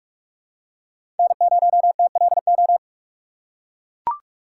Morse code, heard as a keyed tone, about 42 wpm, 700 Hz. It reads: N0TLO
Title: USA Call Signs - 6x Character Spacing - 45wpm